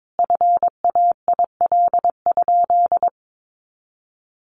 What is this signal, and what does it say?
Morse code, heard as a keyed tone, about 22 wpm, 700 Hz. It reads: FAIL?
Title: Sentences from Top 1400 Words 22wpm